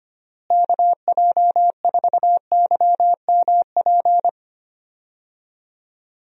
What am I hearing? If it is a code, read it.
Morse code, heard as a keyed tone, about 25 wpm, 700 Hz. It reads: KJ4YMP
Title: USA Call Signs - 6 Characters - Repeated 3x - 25wpm